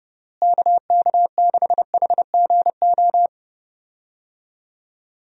Morse code, heard as a keyed tone, about 30 wpm, 700 Hz. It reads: KK6HGO
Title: USA Call Signs - 6 Characters - Repeated 2x - 30wpm